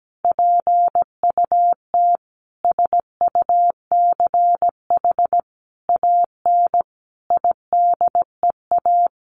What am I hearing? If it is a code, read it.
Morse code, heard as a keyed tone, about 17 wpm, 700 Hz. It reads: PUT SUCH AN IDEA
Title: Sentences from Top 600 Words 17wpm